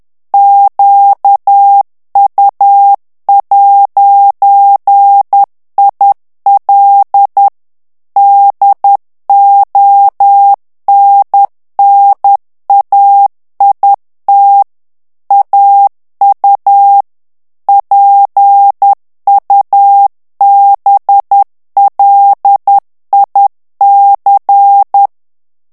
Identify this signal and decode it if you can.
Morse code, heard as a keyed tone, about 10 wpm, 800 Hz. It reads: QU'IL DONNAIT AU PUBLIC
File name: morse mp3 / FRANCAIS TEXTE CLAIR 2 VITESSE 10 MOTS MINUTES